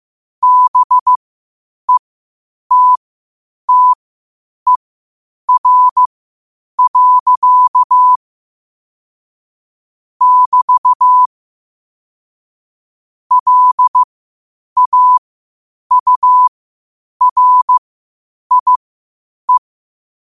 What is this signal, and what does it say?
Morse code, heard as a keyed tone, about 15 wpm, spaced as for 8 wpm, 1000 Hz. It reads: BETTER. <BT> LAURIE
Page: Quotes for Thu, 14 Aug 2025 in Morse Code at 8 words per minute.